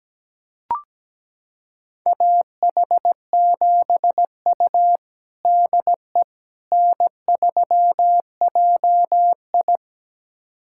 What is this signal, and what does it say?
Morse code, heard as a keyed tone, about 17 wpm, 700 Hz. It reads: AH7U DE N3JI
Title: USA Callsign DE Callsign - 4 Characters - 17wpm